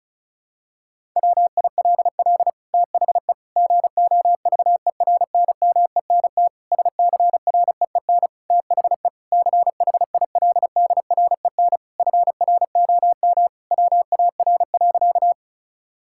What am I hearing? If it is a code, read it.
Morse code, heard as a keyed tone, about 36 wpm, 700 Hz. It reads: WILL THE GOVERNMENT SCREEN THE CHILDREN FROM WAR.
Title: Sentences from Top 1000 Words - Review Part 6 - 35wpm